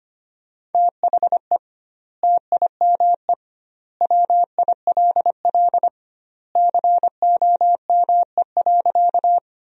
Morse code, heard as a keyed tone, about 25 wpm, 700 Hz. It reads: THE TIME WILL COME.
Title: 2x Word-Spacing - Sentences from Top 100 Words - 25wpm